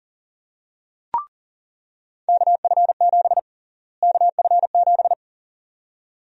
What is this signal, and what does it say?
Morse code, heard as a keyed tone, about 40 wpm, 700 Hz. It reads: KF7 KF7
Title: USA Call Sign Prefixes - Repeated 2x - 40wpm